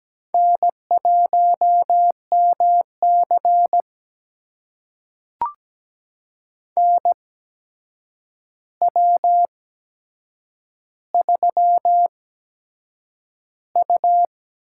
Morse code, heard as a keyed tone, about 17 wpm, spaced as for 7 wpm, 700 Hz. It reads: N1MC NW3U
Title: USA Call Signs - 4 Characters - 8x Character Spacing - 17wpm